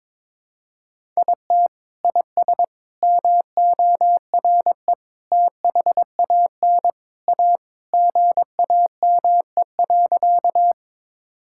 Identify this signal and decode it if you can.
Morse code, heard as a keyed tone, about 22 wpm, 700 Hz. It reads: IT IS MORE THAN A GAME.
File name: practice / 22wpm / Sentences with 3-Word Phrases - 22wpm